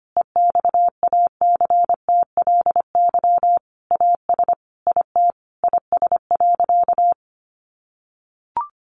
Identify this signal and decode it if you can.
Morse code, heard as a keyed tone, about 25 wpm, 700 Hz. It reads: EXACTLY AS IT IS.
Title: Sentences from Top 900 Words 25wpm